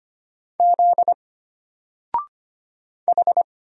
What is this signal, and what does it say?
Morse code, heard as a keyed tone, about 25 wpm, 700 Hz. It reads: Z H
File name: Single-Letter-Number-25wpm